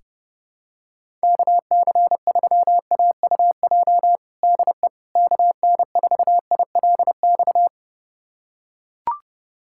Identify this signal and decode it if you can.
Morse code, heard as a keyed tone, about 32 wpm, 700 Hz. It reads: KC3AUJ DE KN4ILX
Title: USA Callsign DE Callsign - 6 Characters - 30wpm